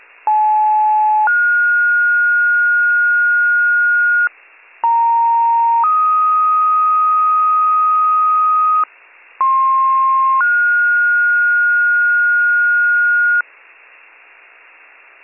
Two-tone page, heard as a one-second tone, then a three-second tone, about 860 then 1430 Hz, about 930 then 1250 Hz, about 1060 then 1430 Hz, with noise. It is heard from 0.3 to 13.4 s.